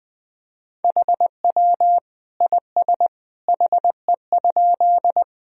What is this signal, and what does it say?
Morse code, heard as a keyed tone, about 20 wpm, 700 Hz. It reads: HW IS HE?
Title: Sentences from Top 100 Words and Abbreviations 20wpm